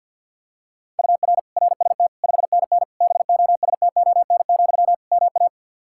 Morse code, heard as a keyed tone, about 50 wpm, 700 Hz. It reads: UR RST 5NN BOSTON, MA
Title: Practice QSO - Exchange 1 - Beginning - 50wpm